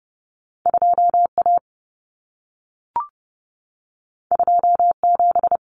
Morse code, heard as a keyed tone, about 30 wpm, 700 Hz. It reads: 2A 27
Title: Common Cut Numbers - 2-Digit 30wpm